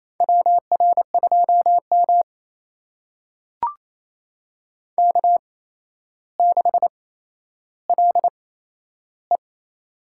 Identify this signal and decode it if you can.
Morse code, heard as a keyed tone, about 28 wpm, spaced as for 12 wpm, 700 Hz. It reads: WR2M K6LE
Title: USA Call Signs - 4 Characters - 8x Character Spacing - 28wpm